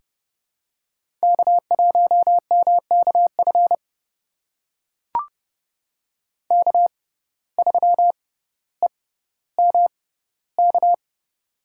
Morse code, heard as a keyed tone, about 32 wpm, spaced as for 16 wpm, 700 Hz. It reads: K1MKF K3EMK
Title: USA Call Signs - 5 Characters - 6x Character Spacing - 30wpm